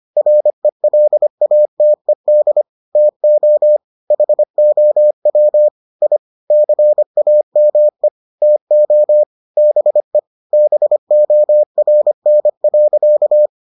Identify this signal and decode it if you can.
Morse code, heard as a keyed tone, about 25 wpm, 600 Hz. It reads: RELATED TO HOW I CAME TO BE BORN.